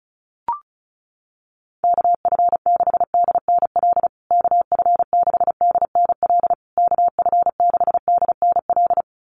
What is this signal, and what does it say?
Morse code, heard as a keyed tone, about 34 wpm, 700 Hz. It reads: KF6DNL KF6DNL KF6DNL
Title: USA Call Signs Repeated 3x 35wpm